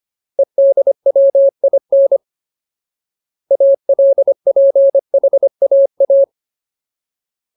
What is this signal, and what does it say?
Morse code, heard as a keyed tone, about 25 wpm, 550 Hz. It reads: EDWIN ALPHAA
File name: Ultimate Word List / 25WPM Ultimate